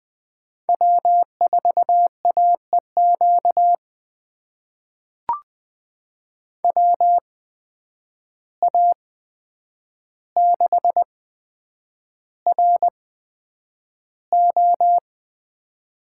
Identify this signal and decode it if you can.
Morse code, heard as a keyed tone, about 20 wpm, spaced as for 9 wpm, 700 Hz. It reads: W4AEQ WA6RO
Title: USA Call Signs - 5 Characters - 8x Character Spacing - 20wpm